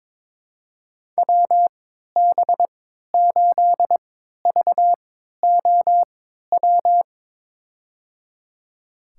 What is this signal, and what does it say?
Morse code, heard as a keyed tone, about 21 wpm, spaced as for 12 wpm, 700 Hz. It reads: WB8VOW
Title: USA Call Signs - 6 Characters - 3x Character Spacing - 22wpm